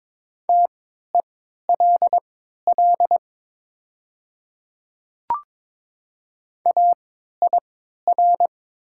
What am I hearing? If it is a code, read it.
Morse code, heard as a keyed tone, about 22 wpm, spaced as for 12 wpm, 700 Hz. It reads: TELL AIR
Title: 3x Character Spacing - Top 200 Words - 22wpm